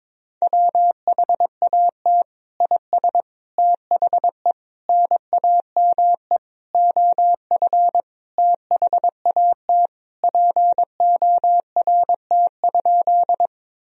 Morse code, heard as a keyed tone, about 22 wpm, 700 Hz. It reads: WHAT IS THE NAME OF THAT PORT?
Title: Sentences from Top 1000 Words - Review Part 1 - 22wpm